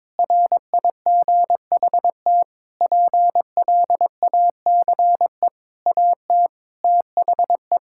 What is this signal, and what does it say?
Morse code, heard as a keyed tone, about 22 wpm, 700 Hz. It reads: RIGHT PLACE AT THE
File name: Sets of 9 Words - 22wpm